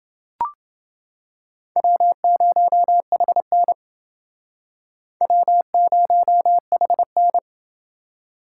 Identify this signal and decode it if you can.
Morse code, heard as a keyed tone, about 29 wpm, 700 Hz. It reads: W0HN W0HN